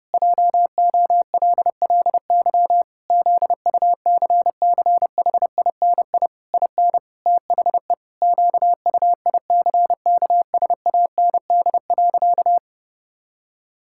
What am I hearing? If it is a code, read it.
Morse code, heard as a keyed tone, about 30 wpm, 700 Hz. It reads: JOLLY ZUCCHINI IN THE QUICKSAND.
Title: Pangram Challenge v2 - 30wpm